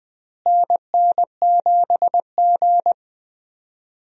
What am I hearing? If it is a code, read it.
Morse code, heard as a keyed tone, about 21 wpm, 700 Hz. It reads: NN7G
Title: USA Call Signs - 4 Characters - Repeated 2x - 20wpm